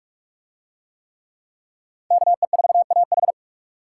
Morse code, heard as a keyed tone, about 41 wpm, 700 Hz. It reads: KE4AH